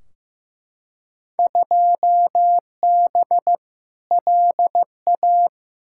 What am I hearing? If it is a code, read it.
Morse code, heard as a keyed tone, about 15 wpm, 700 Hz. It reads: 2B LA
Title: Field Day Exchanges Encore - 15wpm